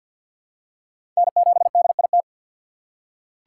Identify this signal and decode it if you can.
Morse code, heard as a keyed tone, about 47 wpm, 690 Hz. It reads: N7DIT
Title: USA Call Signs - 5 Characters - 4x Character Spacing - 50wpm